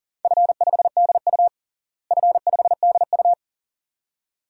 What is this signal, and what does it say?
Morse code, heard as a keyed tone, about 40 wpm, 700 Hz. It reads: F5DU F5DU
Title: DXCC-100 Call Signs - 4 Characters - Repeated 2x - 40wpm